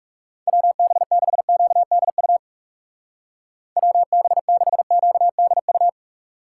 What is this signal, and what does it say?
Morse code, heard as a keyed tone, about 40 wpm, 700 Hz. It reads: WB6QDU WB6QDU